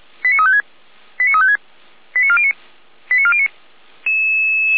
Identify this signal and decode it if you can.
Five-tone selective call (ZVEI), heard as quick runs of five tones, tones of about 70 ms. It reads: 89356 89356 89409 89409